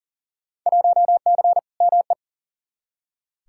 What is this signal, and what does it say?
Morse code, heard as a keyed tone, about 40 wpm, 700 Hz. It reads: 1C ME
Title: Field Day Exchanges - 40wpm